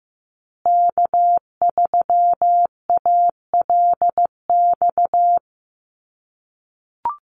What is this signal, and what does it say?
Morse code, heard as a keyed tone, about 15 wpm, 700 Hz. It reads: K3ALX
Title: USA Call Signs Repeated 3x 15wpm